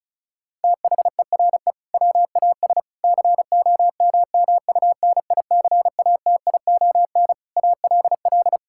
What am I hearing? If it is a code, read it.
Morse code, heard as a keyed tone, about 36 wpm, 700 Hz. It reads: THERE WAS COMMUNICATION ALL
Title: Sentences from Top 1000 Words - Review Part 6 - 35wpm